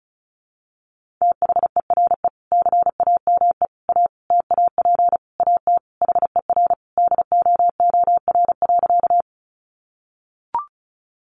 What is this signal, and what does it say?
Morse code, heard as a keyed tone, about 34 wpm, 700 Hz. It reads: THERE CAME A TAP AT HER DOOR.
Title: Sentences from Top 1400 Words 35wpm